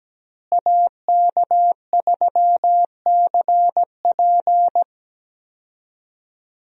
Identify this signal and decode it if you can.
Morse code, heard as a keyed tone, about 17 wpm, 700 Hz. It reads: AK3CP
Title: USA Call Signs - 5 Characters - Repeated 2x - 17wpm